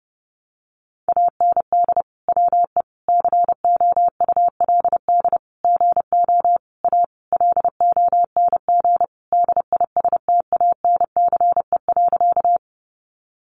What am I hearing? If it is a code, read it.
Morse code, heard as a keyed tone, about 30 wpm, 700 Hz. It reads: AND WE COULD GO A LONG DISTANCE.
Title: Sentences from Top 1200 Words 30wpm